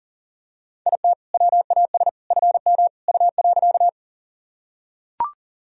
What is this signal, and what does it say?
Morse code, heard as a keyed tone, about 40 wpm, 700 Hz. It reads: IT WAS FM U.
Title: Sentences from Top 100 Words and Abbreviations 40wpm